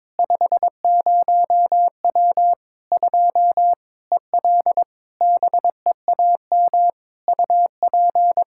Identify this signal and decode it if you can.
Morse code, heard as a keyed tone, about 22 wpm, 700 Hz. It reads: 50W 2 EL BEAM UP
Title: Practice QSO - Exchange 2 - Info - 22wpm